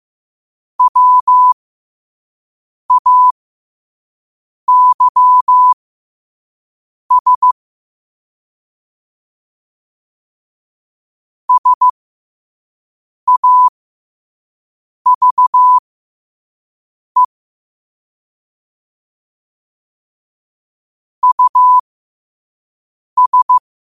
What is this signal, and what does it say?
Morse code, heard as a keyed tone, about 15 wpm, spaced as for 5 wpm, 1000 Hz. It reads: WAYS SAVE US
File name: qotdfiles / 5wpm